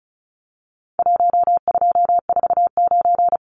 Morse code, heard as a keyed tone, about 34 wpm, 700 Hz. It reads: 1249
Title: Numbers - 4-Digit 35wpm